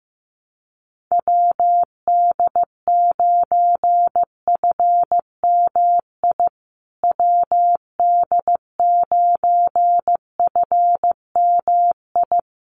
Morse code, heard as a keyed tone, about 15 wpm, 700 Hz. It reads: WD9FMI WD9FMI
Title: USA Call Signs Repeated 3x 15wpm